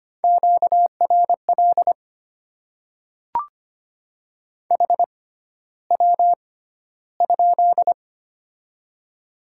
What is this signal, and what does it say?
Morse code, heard as a keyed tone, about 25 wpm, spaced as for 13 wpm, 700 Hz. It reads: QRL HW?
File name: Top 20 QSO Elements - 6x Character Spacing - 25wpm